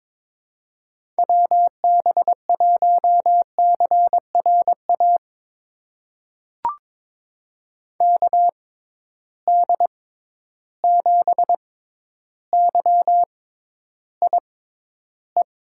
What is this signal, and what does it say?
Morse code, heard as a keyed tone, about 22 wpm, spaced as for 11 wpm, 700 Hz. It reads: WB1CRA KD7YIE
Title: USA Call Signs - 6 Characters - 6x Character Spacing - 22wpm